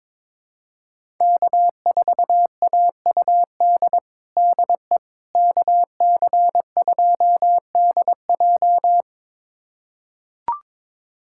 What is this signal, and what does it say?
Morse code, heard as a keyed tone, about 21 wpm, 700 Hz. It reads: K4AUD DE KC2DJ
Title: USA Callsign DE Callsign - 5 Characters - 22wpm